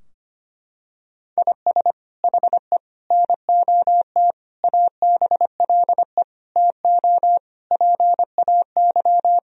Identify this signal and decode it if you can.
Morse code, heard as a keyed tone, about 25 wpm, 700 Hz. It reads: IS HE NOT ABLE TO PAY
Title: Sentences from Top 1000 Words - Review Part 2 - 25wpm